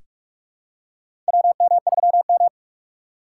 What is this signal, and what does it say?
Morse code, heard as a keyed tone, about 46 wpm, 700 Hz. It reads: WM3M
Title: USA Call Signs - 4 Characters - 4x Character Spacing - 45wpm